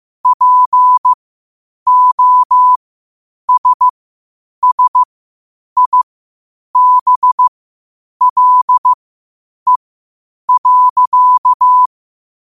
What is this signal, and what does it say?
Morse code, heard as a keyed tone, about 15 wpm, spaced as for 8 wpm, 1000 Hz. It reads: POSSIBLE.
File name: qotdfiles / 8wpm